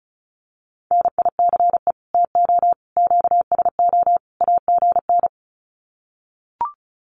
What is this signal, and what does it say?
Morse code, heard as a keyed tone, about 34 wpm, 700 Hz. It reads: NICE TO QSO AGN
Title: Common QSO Phrases 35wpm